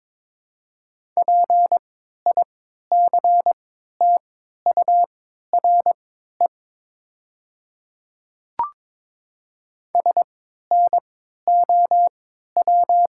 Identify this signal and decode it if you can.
Morse code, heard as a keyed tone, about 22 wpm, spaced as for 12 wpm, 700 Hz. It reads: PICTURE SNOW
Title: 3x Character Spacing - Top 500 Words - 22wpm